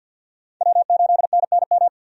Morse code, heard as a keyed tone, about 50 wpm, 700 Hz. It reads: W8NNM
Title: USA Call Signs - 5 Characters - 50wpm